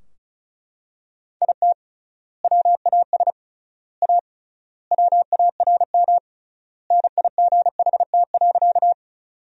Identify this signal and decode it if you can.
Morse code, heard as a keyed tone, about 36 wpm, 700 Hz. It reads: IT WAS A WARM NIGHT.
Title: 3x Word-Spacing - Sentences from Top 500 Words - 35wpm